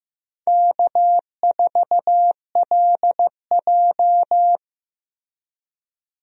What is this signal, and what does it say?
Morse code, heard as a keyed tone, about 15 wpm, 700 Hz. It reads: K4LJ
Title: USA Call Signs - 4 Characters - Repeated 3x - 15wpm